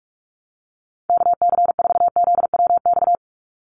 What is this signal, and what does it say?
Morse code, heard as a keyed tone, about 46 wpm, 690 Hz. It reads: KC4ZWX
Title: USA Call Signs 45wpm